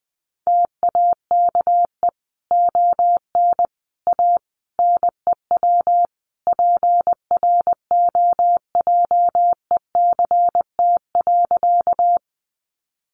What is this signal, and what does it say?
Morse code, heard as a keyed tone, about 20 wpm, 700 Hz. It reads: TAKE ON A NEW PROJECT.
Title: Sentences from Top 1400 Words 20wpm